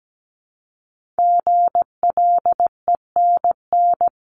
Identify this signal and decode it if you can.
Morse code, heard as a keyed tone, about 17 wpm, 700 Hz. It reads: GLENN
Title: CWT Exchange 17wpm